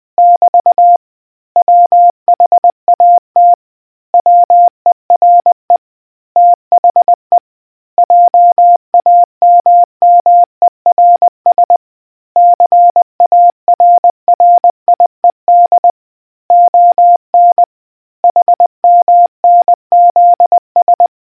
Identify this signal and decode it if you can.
Morse code, heard as a keyed tone, about 20 wpm, 700 Hz. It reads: <BT> WHAT WERE THE JAMMERS CARRIED ON HMNZS